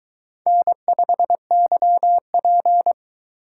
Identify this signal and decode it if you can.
Morse code, heard as a keyed tone, about 22 wpm, 700 Hz. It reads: N5YP